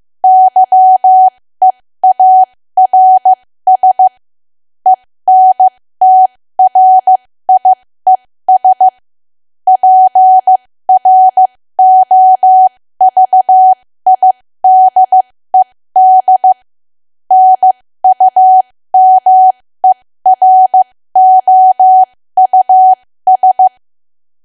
Morse code, heard as a keyed tone, about 15 wpm, spaced as for 13 wpm, 750 Hz. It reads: YEARS ENTRIES PROVIDED NUMEROUS
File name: cwa / W1AW-Level2 / 13WPM